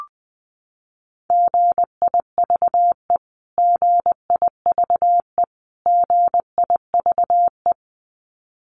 Morse code, heard as a keyed tone, about 20 wpm, 700 Hz. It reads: GIVE GIVE GIVE
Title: Top 500 Words Repeated 20wpm